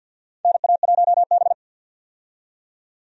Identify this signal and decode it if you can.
Morse code, heard as a keyed tone, about 47 wpm, 700 Hz. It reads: NA1B
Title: USA Call Signs - 4 Characters - 50wpm